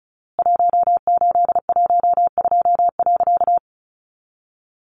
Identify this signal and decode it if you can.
Morse code, heard as a keyed tone, about 34 wpm, 700 Hz. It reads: 1812.
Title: A Sound Of Thunder 35wpm